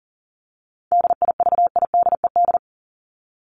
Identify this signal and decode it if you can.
Morse code, heard as a keyed tone, about 40 wpm, 700 Hz. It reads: DIVIDED
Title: ED Suffix - Top 100 Words with Base Words 40wpm